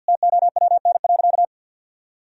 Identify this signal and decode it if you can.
Morse code, heard as a keyed tone, about 50 wpm, 700 Hz. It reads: TOWN.
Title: Sentences from Top 1000 Words - Review Part 2 - 50wpm